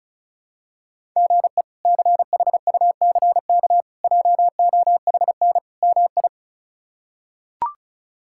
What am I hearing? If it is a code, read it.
Morse code, heard as a keyed tone, about 36 wpm, 700 Hz. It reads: GE CHUCK JOHN MI
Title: K1USN SST QSO Practice - 35wpm